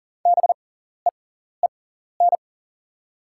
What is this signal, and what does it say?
Morse code, heard as a keyed tone, about 40 wpm, spaced as for 14 wpm, 700 Hz. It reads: BEEN
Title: 6x Character Spacing - Top 100 Words - 40wpm